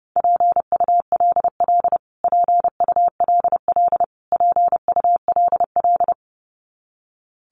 Morse code, heard as a keyed tone, about 28 wpm, 700 Hz. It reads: PULL PULL PULL
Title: Top 500 Words Repeated 30wpm